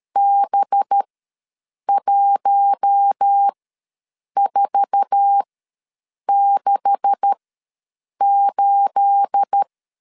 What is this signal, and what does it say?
Morse code, heard as a keyed tone, about 13 wpm, spaced as for 7 wpm, 800 Hz. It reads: B1468
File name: LEZIONI / 9 286 ws 12wpm 3351